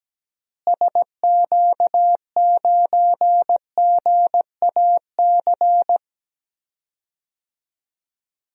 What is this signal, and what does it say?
Morse code, heard as a keyed tone, about 17 wpm, 700 Hz. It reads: SQ9GAC
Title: DXCC-100 Call Signs - 6 Characters - Repeated 3x - 17wpm